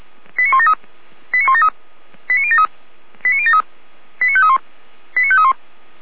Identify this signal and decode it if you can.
Five-tone selective call (ZVEI), heard as quick runs of five tones, tones of about 70 ms. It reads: 89173 89173 89073 89073 89531 89531